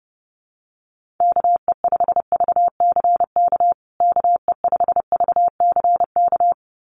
Morse code, heard as a keyed tone, about 30 wpm, 700 Hz. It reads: KE5VCK KE5VCK
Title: USA Call Signs Repeated 2x 30wpm